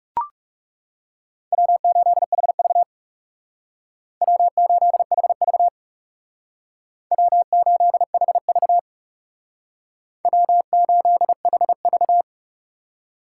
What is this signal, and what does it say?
Morse code, heard as a keyed tone, about 38 wpm, 700 Hz. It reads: W8HV W8HV W8HV W8HV